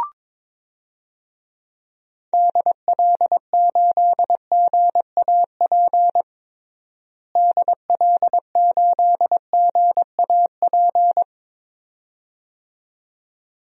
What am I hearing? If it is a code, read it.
Morse code, heard as a keyed tone, about 22 wpm, 700 Hz. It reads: DL8GAP DL8GAP
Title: DXCC-100 Call Signs - 6 Characters - Repeated 2x - 22wpm